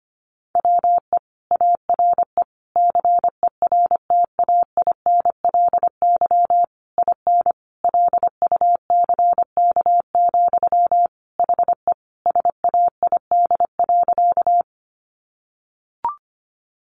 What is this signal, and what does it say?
Morse code, heard as a keyed tone, about 25 wpm, 700 Hz. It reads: WE ARE CERTAINLY IN LUCK, HE SAID.
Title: Sentences from Top 1500 Words 25wpm